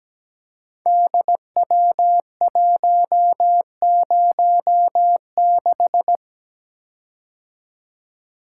Morse code, heard as a keyed tone, about 17 wpm, 700 Hz. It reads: DW106
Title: Code Groups - 5 Letter:Numbers - 17wpm